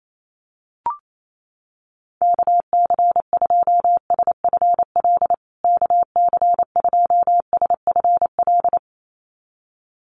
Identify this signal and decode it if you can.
Morse code, heard as a keyed tone, about 28 wpm, 700 Hz. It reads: KC2SFL KC2SFL
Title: USA Call Signs Repeated 2x 28wpm